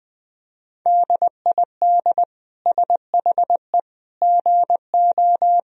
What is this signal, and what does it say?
Morse code, heard as a keyed tone, about 20 wpm, 700 Hz. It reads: DID SHE GO
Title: Sentences from Top 1000 Words - Review Part 3 - 20wpm